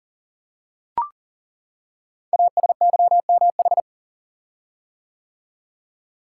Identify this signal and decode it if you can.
Morse code, heard as a keyed tone, about 40 wpm, 700 Hz. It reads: ASYMH
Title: Code Groups - 5 Letters - 40wpm